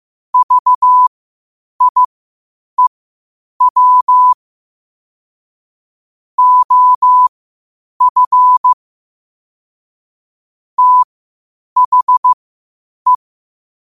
Morse code, heard as a keyed tone, about 15 wpm, spaced as for 8 wpm, 1000 Hz. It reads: VIEW OF THE
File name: qotdfiles / 8wpm